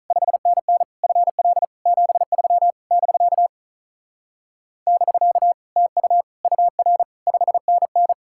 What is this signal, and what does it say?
Morse code, heard as a keyed tone, about 44 wpm, 700 Hz. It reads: 5NN FL 73 <BK> <BK> TU UR 5NN
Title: Speed-Racing Common QSO Phrases - 35wpm